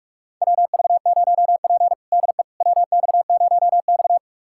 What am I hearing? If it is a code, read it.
Morse code, heard as a keyed tone, about 46 wpm, 700 Hz. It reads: WV0P DE WX0X
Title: USA Callsign DE Callsign - 4 Characters - 45wpm